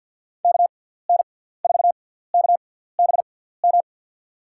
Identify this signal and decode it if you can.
Morse code, heard as a keyed tone, about 53 wpm, spaced as for 17 wpm, 690 Hz. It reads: KN4KBM
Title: USA Call Signs - 6x Character Spacing - 50wpm